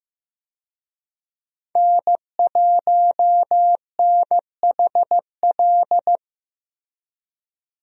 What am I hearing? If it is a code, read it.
Morse code, heard as a keyed tone, about 15 wpm, 700 Hz. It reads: N1NHL
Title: USA Call Signs - 5 Characters - Repeated 2x - 15wpm